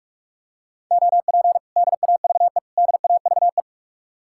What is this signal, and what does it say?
Morse code, heard as a keyed tone, about 47 wpm, 700 Hz. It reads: OP DAVE DAVE